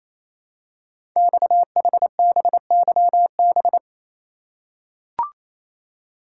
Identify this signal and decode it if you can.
Morse code, heard as a keyed tone, about 27 wpm, 700 Hz. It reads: XHBYB
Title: Code Groups - 5 Letters - 28wpm